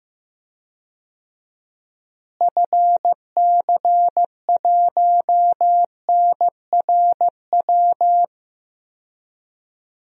Morse code, heard as a keyed tone, about 15 wpm, 700 Hz. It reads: FC1NRW
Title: DXCC-100 Call Signs - 6 Characters - Repeated 3x - 15wpm